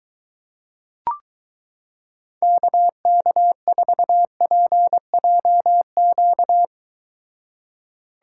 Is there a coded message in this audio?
Yes: Morse code from 2.4 to 6.6 s, heard as a keyed tone, about 23 wpm, 700 Hz. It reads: KK4PJQ